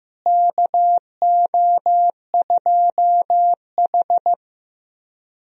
Morse code, heard as a keyed tone, about 15 wpm, 700 Hz. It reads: KO2H
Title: USA Call Signs - 4 Characters - Repeated 3x - 15wpm